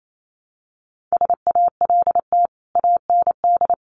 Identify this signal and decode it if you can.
Morse code, heard as a keyed tone, about 28 wpm, 700 Hz. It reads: SALT AND
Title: Sets of 5 Words Encore 28wpm